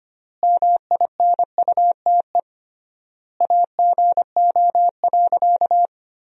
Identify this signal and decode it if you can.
Morse code, heard as a keyed tone, about 25 wpm, 700 Hz. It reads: MINUTE AGO.
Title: 3x Word-Spacing - Sentences from Top 500 Words - 25wpm